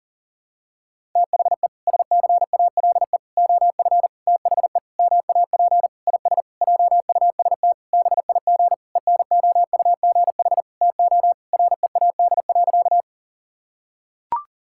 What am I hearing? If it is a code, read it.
Morse code, heard as a keyed tone, about 40 wpm, 700 Hz. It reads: THE SCALE OF THE MAP IS JUST BIG ENOUGH TO READ.
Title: Sentences from Top 1000 Words - Review Part 5 - 40wpm